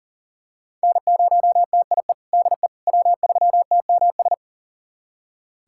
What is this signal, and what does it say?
Morse code, heard as a keyed tone, about 40 wpm, 700 Hz. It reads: N0TIE DE W3TMS
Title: USA Callsign DE Callsign - 5 Characters - 40wpm